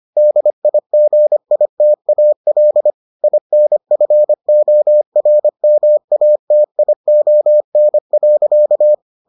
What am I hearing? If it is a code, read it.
Morse code, heard as a keyed tone, about 25 wpm, 600 Hz. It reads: DIGITAL INFORMATION.